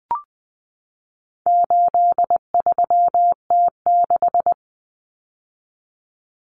Morse code, heard as a keyed tone, about 20 wpm, 700 Hz. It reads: 83T6
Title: Common Cut Numbers - 4-Digit 20wpm